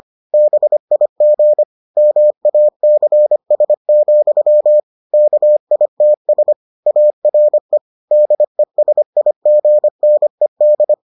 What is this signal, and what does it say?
Morse code, heard as a keyed tone, about 25 wpm, 600 Hz. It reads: BIG MACS, KITS ARE DESIGNED